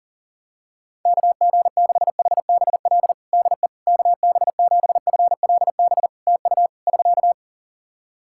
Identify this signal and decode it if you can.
Morse code, heard as a keyed tone, about 38 wpm, 700 Hz. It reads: KG6HBL DE KB7FLB TU <SK>
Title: Practice QSO - Exchange 4 - Ending - 40wpm